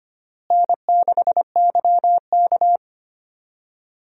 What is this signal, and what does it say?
Morse code, heard as a keyed tone, about 25 wpm, 700 Hz. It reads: N6YK
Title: USA Call Signs - 4 Characters - Repeated 3x - 25wpm